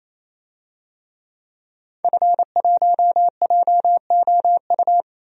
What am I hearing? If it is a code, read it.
Morse code, heard as a keyed tone, about 28 wpm, 700 Hz. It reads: F1JOU
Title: DXCC-100 Call Signs - 5 Characters - 28wpm